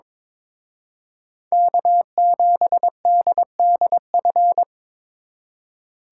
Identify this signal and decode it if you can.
Morse code, heard as a keyed tone, about 26 wpm, 700 Hz. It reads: K7DDF
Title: Speed Racing - USA 1x3 Call Signs 22wpm